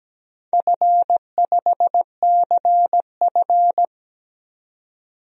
Morse code, heard as a keyed tone, about 17 wpm, 700 Hz. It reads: F5CF
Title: DXCC-100 Call Signs - 4 Characters - 17wpm